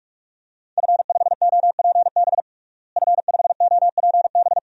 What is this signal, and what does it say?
Morse code, heard as a keyed tone, about 42 wpm, 700 Hz. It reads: F5OPB F5OPB
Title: DXCC-100 Call Signs - 5 Characters - Repeated 2x - 45wpm